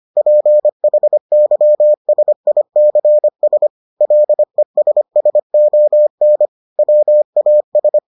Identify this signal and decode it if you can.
Morse code, heard as a keyed tone, about 25 wpm, 600 Hz. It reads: PHYSICS LESSON WAS